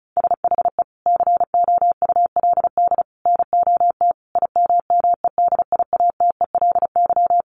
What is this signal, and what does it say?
Morse code, heard as a keyed tone, about 34 wpm, 700 Hz. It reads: SHE COULD NOT IMMEDIATELY
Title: Sentences from Top 1100 Words 35wpm